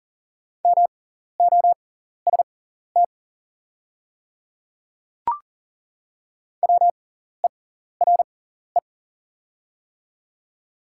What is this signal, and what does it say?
Morse code, heard as a keyed tone, about 38 wpm, spaced as for 14 wpm, 690 Hz. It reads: MOST WERE
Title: 6x Character Spacing - Top 100 Words - 40wpm